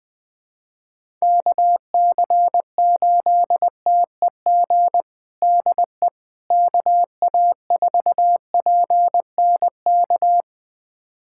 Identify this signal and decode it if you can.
Morse code, heard as a keyed tone, about 21 wpm, 700 Hz. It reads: KC8TEG DE KA4PNK
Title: USA Callsign DE Callsign 20wpm